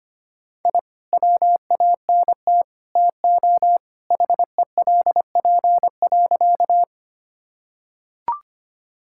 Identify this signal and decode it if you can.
Morse code, heard as a keyed tone, about 25 wpm, 700 Hz. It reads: I WANT TO HELP.
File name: practice / 25wpm / Sentences with 3-Word Phrases - 25wpm